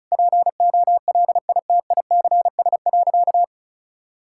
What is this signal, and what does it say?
Morse code, heard as a keyed tone, about 36 wpm, 700 Hz. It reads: POLITICS.
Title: Sentences from Top 1000 Words - Review Part 6 - 35wpm